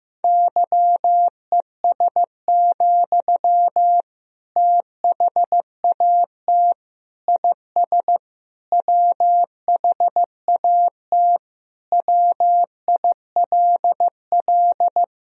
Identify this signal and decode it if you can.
Morse code, heard as a keyed tone, about 16 wpm, 700 Hz. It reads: YES, THAT IS WHAT WILL
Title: Sentences from Top 500 Words and Abbreviations 15wpm